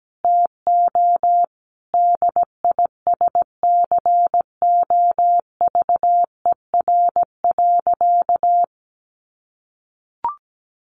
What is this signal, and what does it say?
Morse code, heard as a keyed tone, about 17 wpm, 700 Hz. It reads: TO DISCOVER.
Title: Sentences from Top 1100 Words 17wpm